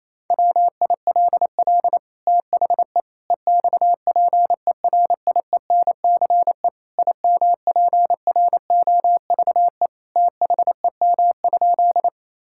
Morse code, heard as a keyed tone, about 27 wpm, 700 Hz. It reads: WILL THE EXPERIENCE IMPROVE THEM?
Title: Sentences from Top 1000 Words - Review Part 5 - 28wpm